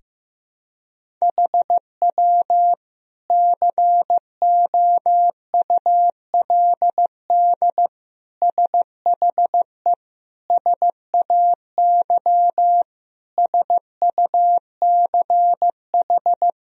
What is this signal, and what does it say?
Morse code, heard as a keyed tone, about 15 wpm, 700 Hz. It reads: HW COULD SHE SAY SUCH
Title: Sentences from Top 200 Words and Abbreviations 15wpm